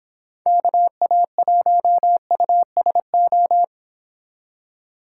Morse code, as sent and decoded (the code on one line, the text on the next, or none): -.- .- .---- ..- ... ---
KA1USO